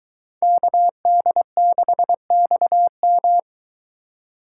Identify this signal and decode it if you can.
Morse code, heard as a keyed tone, about 23 wpm, 700 Hz. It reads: KD6XM